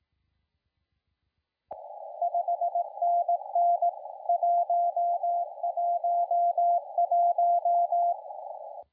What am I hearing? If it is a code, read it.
Morse code, heard as a keyed tone, about 18 wpm, 690 Hz, with radio noise. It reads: 5NN 111
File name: la4c / sac-cw-2017 / CONTEST AUDIO 1100Z